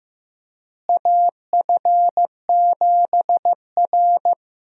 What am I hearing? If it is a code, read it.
Morse code, heard as a keyed tone, about 15 wpm, 700 Hz. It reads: AF7R
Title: USA Call Signs - 4 Characters - Repeated 2x - 15wpm